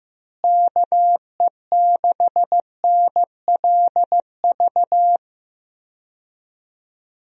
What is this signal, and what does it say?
Morse code, heard as a keyed tone, about 15 wpm, 700 Hz. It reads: KE6NLV
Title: USA Call Signs - 6 Characters - Repeated 3x - 15wpm